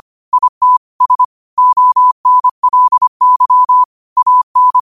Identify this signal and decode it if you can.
Morse code, heard as a keyed tone, about 25 wpm, 1000 Hz. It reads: IT S ONLY AN
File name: qotdfiles / 25wpm